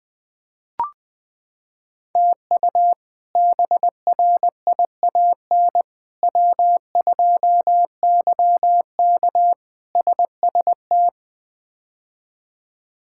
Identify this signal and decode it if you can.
Morse code, heard as a keyed tone, about 20 wpm, 700 Hz. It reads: TU BRIAN W2YK SST
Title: K1USN SST QSO Practice - 20wpm